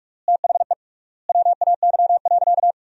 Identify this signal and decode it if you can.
Morse code, heard as a keyed tone, about 46 wpm, 700 Hz. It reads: THE WAY.
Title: 3x Word-Spacing - Sentences from Top 300 Words - 45wpm